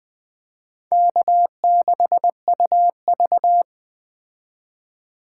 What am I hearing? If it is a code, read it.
Morse code, heard as a keyed tone, about 20 wpm, 700 Hz. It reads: K6UV